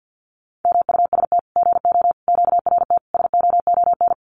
Morse code, heard as a keyed tone, about 53 wpm, 700 Hz. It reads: MUST GO QRT SOON
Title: Common QSO Phrases 50wpm